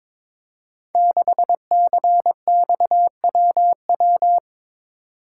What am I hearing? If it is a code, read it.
Morse code, heard as a keyed tone, about 22 wpm, 700 Hz. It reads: 6CXWW
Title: Code Groups - 5 Letter:Numbers - 22wpm